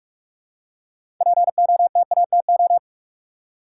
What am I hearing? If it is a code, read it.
Morse code, heard as a keyed tone, about 46 wpm, 700 Hz. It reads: POTATO
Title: Six Letter Words - 45wpm